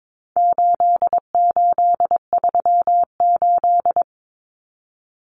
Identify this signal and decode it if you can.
Morse code, heard as a keyed tone, about 22 wpm, 700 Hz. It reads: 8838
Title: Numbers - 4-Digit 22wpm